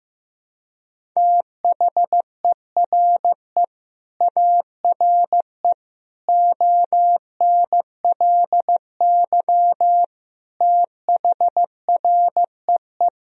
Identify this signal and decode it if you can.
Morse code, heard as a keyed tone, about 15 wpm, 700 Hz. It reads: THERE ARE ONLY THREE
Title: Sentences from Top 1000 Words - Review Part 1 - 15wpm